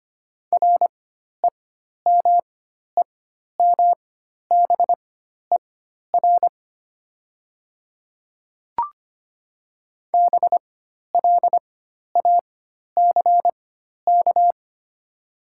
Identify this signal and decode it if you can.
Morse code, heard as a keyed tone, about 25 wpm, spaced as for 12 wpm, 700 Hz. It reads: REMEMBER BLACK
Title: 4x Character Spacing - Top 500 Words - 25wpm